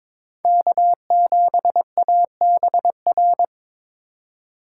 Morse code, heard as a keyed tone, about 22 wpm, 700 Hz. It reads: K7ABR
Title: USA Call Signs - 5 Characters - 3x Character Spacing - 22wpm